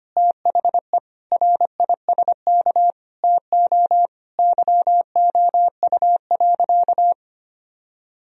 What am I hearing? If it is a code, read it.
Morse code, heard as a keyed tone, about 25 wpm, 700 Hz. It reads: THE RISK TO YOU.
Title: Sentences from Top 1000 Words - Review Part 4 - 25wpm